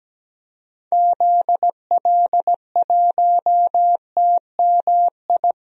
Morse code, heard as a keyed tone, about 17 wpm, 700 Hz. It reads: ZL1TMI
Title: DXCC-100 Call Signs - 6 Characters - 17wpm